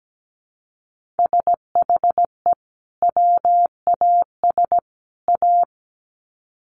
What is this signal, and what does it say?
Morse code, heard as a keyed tone, about 17 wpm, 700 Hz. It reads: SHE WAS A
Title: Sets of 3 Words 17wpm